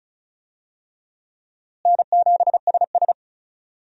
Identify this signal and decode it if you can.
Morse code, heard as a keyed tone, about 36 wpm, 700 Hz. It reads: N7SS
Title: USA Call Signs - 4 Characters - Repeated 2x - 35wpm